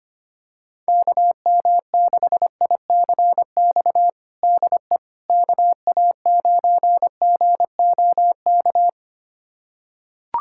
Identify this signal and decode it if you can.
Morse code, heard as a keyed tone, about 25 wpm, 700 Hz. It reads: KM6ICX DE KA9GOK
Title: USA Callsign DE Callsign - 6 Characters - 25wpm